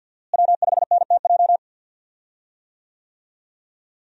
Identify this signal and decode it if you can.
Morse code, heard as a keyed tone, about 49 wpm, 690 Hz. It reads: W5NTJ
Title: USA Call Signs - 5 Characters - Repeated 3x - 50wpm